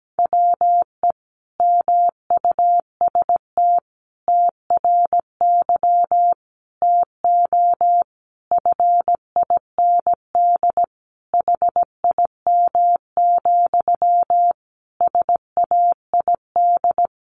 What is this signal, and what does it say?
Morse code, heard as a keyed tone, about 17 wpm, 700 Hz. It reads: WE MUST TRY TO FIND HIM, SAID
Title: Sentences from Top 400 Words 17wpm